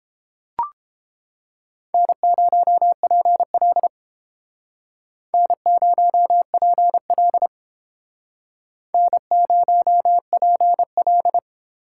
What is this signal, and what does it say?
Morse code, heard as a keyed tone, about 31 wpm, 700 Hz. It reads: N0PL N0PL N0PL